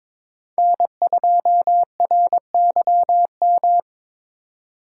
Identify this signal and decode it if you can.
Morse code, heard as a keyed tone, about 26 wpm, 700 Hz. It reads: N2RYM
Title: Speed Racing - USA 1x3 Call Signs 22wpm